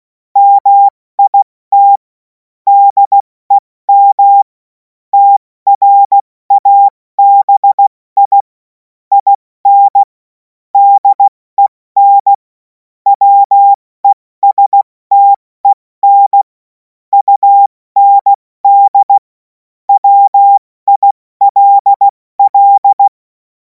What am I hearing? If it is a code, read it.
Morse code, heard as a keyed tone, about 16 wpm, 800 Hz. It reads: MIT DEM TRABI IN DEN WESTEN UND WILL
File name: Morsekurs / Witze / Witz 28. April 2026 70 F80